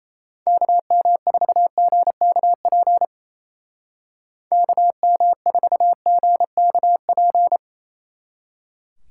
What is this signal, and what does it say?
Morse code, heard as a keyed tone, about 34 wpm, 700 Hz. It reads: KM4GKP KM4GKP
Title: Speed Racing - USA Call Signs 28wpm